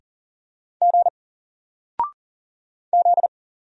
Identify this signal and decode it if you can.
Morse code, heard as a keyed tone, about 39 wpm, 700 Hz. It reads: G Z
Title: Single Letter-Number 40wpm